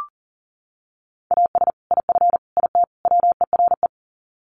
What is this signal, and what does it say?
Morse code, heard as a keyed tone, about 37 wpm, 700 Hz. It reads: AS IF IT WERE
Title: Sets of 4 Words 40wpm